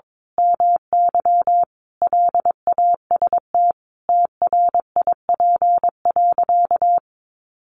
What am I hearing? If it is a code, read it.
Morse code, heard as a keyed tone, about 22 wpm, 700 Hz. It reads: MY LAST TRIP.
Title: Sentences from Top 1300 Words 22wpm